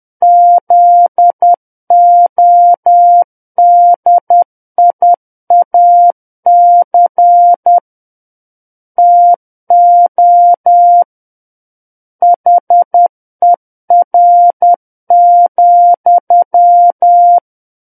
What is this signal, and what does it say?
Morse code, heard as a keyed tone, about 10 wpm, 700 Hz. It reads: ZODIAC TO HER,
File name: Book One / 01-Chapter 1 - 10 WPM